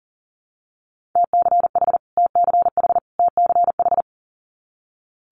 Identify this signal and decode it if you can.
Morse code, heard as a keyed tone, about 42 wpm, 700 Hz. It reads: TCH TCH TCH
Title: Multi-letter Phonemes Repeated 3x 40wpm